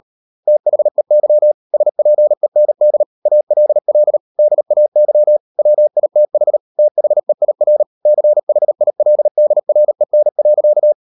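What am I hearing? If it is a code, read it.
Morse code, heard as a keyed tone, about 38 wpm, 600 Hz. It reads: THEY SPEND ALL DAY WITH THEIR CHILDREN.
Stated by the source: Morse code practice at 38 WPM, sending Ken Follett's novel Never in English, from CW med Gnister.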